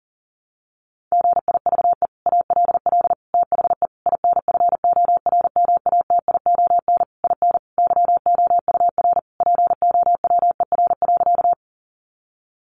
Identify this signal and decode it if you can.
Morse code, heard as a keyed tone, about 40 wpm, 700 Hz. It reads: GIVE ALL THE INFORMATION IN YOUR POWER.
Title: Sentences from Top 600 Words 40wpm